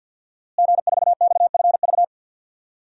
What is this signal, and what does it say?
Morse code, heard as a keyed tone, about 51 wpm, 700 Hz. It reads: G4XFV
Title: DXCC-100 Call Signs - 5 Characters - Repeated 3x - 50wpm